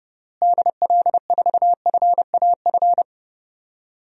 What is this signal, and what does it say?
Morse code, heard as a keyed tone, about 30 wpm, 700 Hz. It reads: DL4FAF
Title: DXCC-100 Call Signs - 6 Characters - Repeated 3x - 30wpm